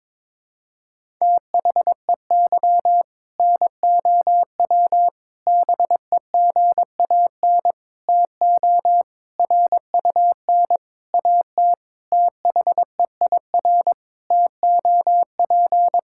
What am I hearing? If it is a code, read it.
Morse code, heard as a keyed tone, about 22 wpm, 700 Hz. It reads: THEY NOW BEGAN TO RUN AT THEIR TOP
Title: Sentences from Top 1000 Words - Review Part 5 - 22wpm